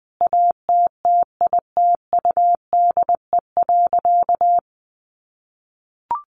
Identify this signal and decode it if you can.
Morse code, heard as a keyed tone, about 20 wpm, 700 Hz. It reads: ATTITUDE.
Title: Sentences from Top 1300 Words 20wpm